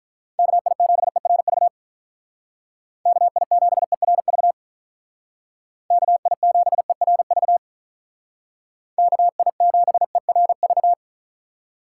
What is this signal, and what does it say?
Morse code, heard as a keyed tone, about 44 wpm, 700 Hz. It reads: KI7ERV KI7ERV KI7ERV KI7ERV